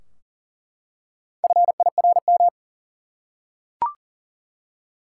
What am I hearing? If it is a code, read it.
Morse code, heard as a keyed tone, about 40 wpm, 700 Hz. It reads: FIRM
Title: Four Letter Words - 40wpm